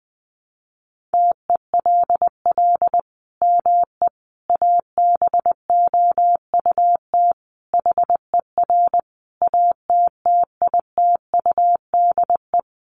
Morse code, heard as a keyed tone, about 20 wpm, 700 Hz. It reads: TELL ME ABOUT HER ATTITUDE
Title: Sentences from Top 1300 Words 20wpm